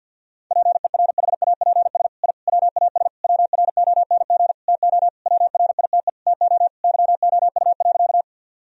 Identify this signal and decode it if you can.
Morse code, heard as a keyed tone, about 46 wpm, 700 Hz. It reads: PERHAPS I WAS WRONG TO WRITE TO YOU.
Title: Sentences from Top 1000 Words - Review Part 1 - 50wpm